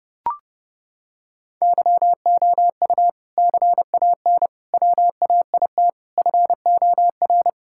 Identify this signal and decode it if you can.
Morse code, heard as a keyed tone, about 30 wpm, 700 Hz. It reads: YOU CAN WAIT FOR
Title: Sentences from Top 1000 Words - Review Part 6 - 30wpm